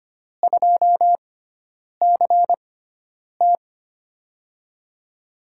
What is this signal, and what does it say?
Morse code, heard as a keyed tone, about 25 wpm, spaced as for 9 wpm, 700 Hz. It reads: 2CT
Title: USA Call Signs - 6x Character Spacing - 25wpm